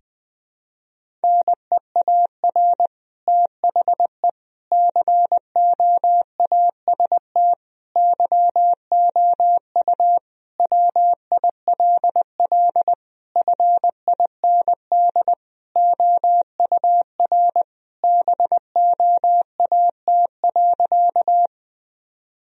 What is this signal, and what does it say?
Morse code, heard as a keyed tone, about 20 wpm, 700 Hz. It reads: NEAR THE COAST YOU WILL FIND OUR BOAT.
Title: Sentences from Top 1000 Words - Review Part 4 - 20wpm